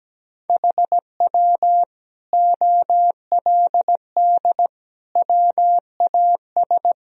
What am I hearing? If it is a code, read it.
Morse code, heard as a keyed tone, about 17 wpm, 700 Hz. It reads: HW OLD WAS
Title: Sentences from Top 200 Words and Abbreviations 17wpm